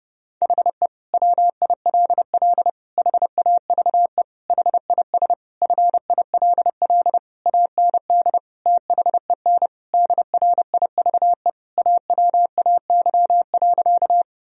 Morse code, heard as a keyed tone, about 30 wpm, 700 Hz. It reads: HE WILL HAVE HIS FILL AND THEN DRIVE AWAY.
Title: Sentences from Top 1000 Words - Review Part 3 - 30wpm